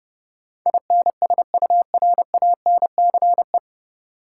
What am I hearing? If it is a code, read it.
Morse code, heard as a keyed tone, about 32 wpm, 700 Hz. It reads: INSURANCE
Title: Sentences from Top 1100 Words - New Word Drill - 30wpm